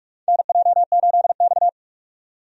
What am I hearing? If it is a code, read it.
Morse code, heard as a keyed tone, about 46 wpm, 700 Hz. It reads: NJ8X
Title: USA Call Signs - 4 Characters - 45wpm